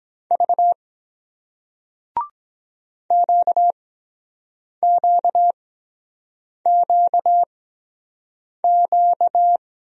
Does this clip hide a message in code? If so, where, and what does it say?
Yes: Morse code from 0.3 to 9.6 s, heard as a keyed tone, about 23 wpm, 700 Hz. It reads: V Q Q Q Q